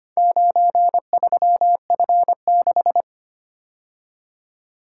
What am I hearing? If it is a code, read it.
Morse code, heard as a keyed tone, about 25 wpm, 700 Hz. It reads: 93F6
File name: practice / 25wpm / Four Letter-Number Combinations - 25wpm